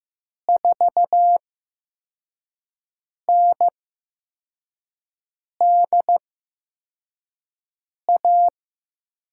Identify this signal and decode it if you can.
Morse code, heard as a keyed tone, about 15 wpm, spaced as for 4 wpm, 700 Hz. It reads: 4NDA
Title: USA Call Signs - 6 Characters - 8x Character Spacing - 15wpm